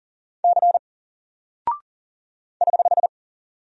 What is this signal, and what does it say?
Morse code, heard as a keyed tone, about 40 wpm, 700 Hz. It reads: C <HH>
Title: Single Character (Letter:Number:Punctuation:Prosign) Alt 40wpm